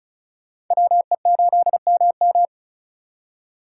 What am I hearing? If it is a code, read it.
Morse code, heard as a keyed tone, about 44 wpm, 700 Hz. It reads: WE8MM
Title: Speed Racing - USA 2x2 Call Signs 35wpm